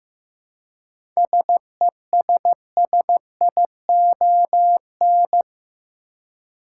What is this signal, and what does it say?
Morse code, heard as a keyed tone, about 15 wpm, 700 Hz. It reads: SESSION
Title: Seven Letter Words - 15wpm